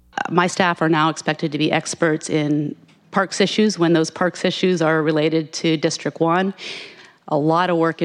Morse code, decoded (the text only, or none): none